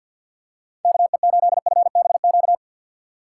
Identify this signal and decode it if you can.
Morse code, heard as a keyed tone, about 53 wpm, 700 Hz. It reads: KE8FBQ
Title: USA Call Signs - 6 Characters - 6x Character Spacing - 50wpm